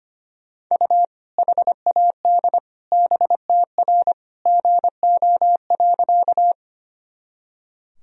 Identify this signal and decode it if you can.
Morse code, heard as a keyed tone, about 25 wpm, 700 Hz. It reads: U HAD BTR GO.
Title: Sentences from Top 400 Words and Abbreviations 25wpm